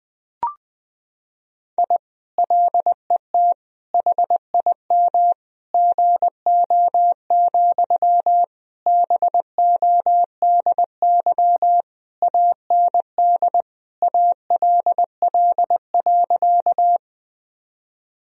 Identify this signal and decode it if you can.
Morse code, heard as a keyed tone, about 21 wpm, 700 Hz. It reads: I LET HIM GO, BODY AND ALL.
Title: Sentences from Top 1000 Words - Review Part 2 - 20wpm